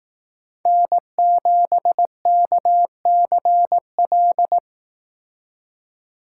Morse code, heard as a keyed tone, about 18 wpm, 700 Hz. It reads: N7KCL
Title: Speed Racing - USA 1x3 Call Signs 15wpm